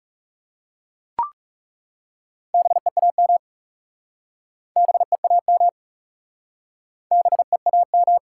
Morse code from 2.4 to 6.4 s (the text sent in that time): BEAM BEAM